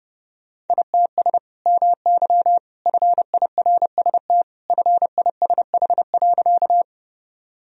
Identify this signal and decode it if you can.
Morse code, heard as a keyed tone, about 29 wpm, 700 Hz. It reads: ITS MY FIRST FISH.
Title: Sentences from Top 1000 Words - Review Part 1 - 30wpm